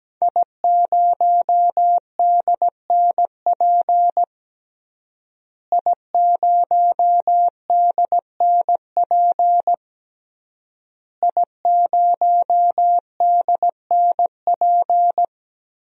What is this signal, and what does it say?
Morse code, heard as a keyed tone, about 17 wpm, 700 Hz. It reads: I0DNP I0DNP I0DNP
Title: DXCC-100 Call Signs - 5 Characters - Repeated 3x - 17wpm